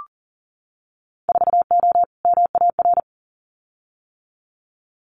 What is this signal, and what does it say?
Morse code, heard as a keyed tone, about 40 wpm, 700 Hz. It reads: 4O MAR
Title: Winter Field Day Exchanges Encore 40wpm